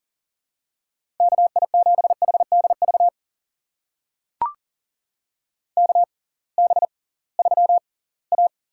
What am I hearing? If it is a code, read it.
Morse code, heard as a keyed tone, about 40 wpm, spaced as for 23 wpm, 700 Hz. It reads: KI7HDV KB3A
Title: USA Call Signs - 6 Characters - 6x Character Spacing - 40wpm